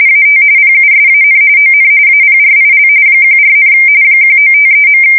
RTTY, about 45.45 baud. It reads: AT PUSAN ON APRIL 14, 1953.
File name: fsk170 / battleshipnj